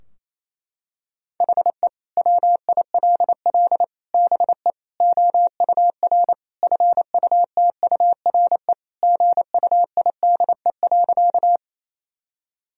Morse code, heard as a keyed tone, about 28 wpm, 700 Hz. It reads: HE WILL BE OUR FUTURE GUIDE.
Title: Sentences from Top 1000 Words - Review Part 3 - 28wpm